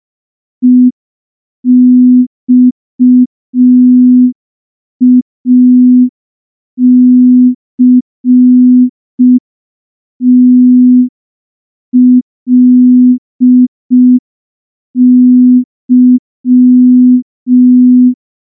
Morse code, heard as a keyed tone, about 5 wpm, 250 Hz. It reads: EXACTLY
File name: morse / morse-training-05WPM